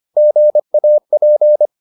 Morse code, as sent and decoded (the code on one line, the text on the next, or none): --. .- .--.
GAP